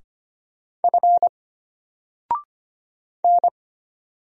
Morse code, heard as a keyed tone, about 25 wpm, 700 Hz. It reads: F N
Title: Single Letter-Number Alt 25wpm